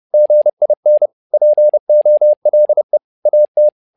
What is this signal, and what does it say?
Morse code, heard as a keyed tone, about 30 wpm, 600 Hz. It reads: GIN POLE AT
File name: learning center / tower trouble / tower trouble 30wpm